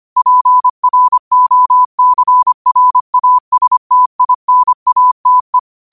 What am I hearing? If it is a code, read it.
Morse code, heard as a keyed tone, about 25 wpm, 1000 Hz. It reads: PROCRASTINATE
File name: qotdfiles / 25wpm